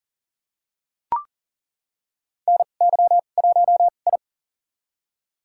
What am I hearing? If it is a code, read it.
Morse code, heard as a keyed tone, about 38 wpm, spaced as for 28 wpm, 700 Hz. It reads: NY1I
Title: USA Call Signs - 4 Characters - 2x Character Spacing - 40wpm